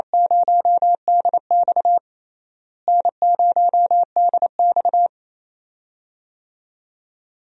Morse code, heard as a keyed tone, about 28 wpm, 700 Hz. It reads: 0DX N0DX
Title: USA Call Signs - 4 Characters - Repeated 2x - 28wpm